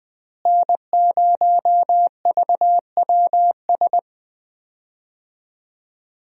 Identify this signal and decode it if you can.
Morse code, heard as a keyed tone, about 20 wpm, 700 Hz. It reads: N0VWS
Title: USA Call Signs - 5 Characters - Repeated 3x - 20wpm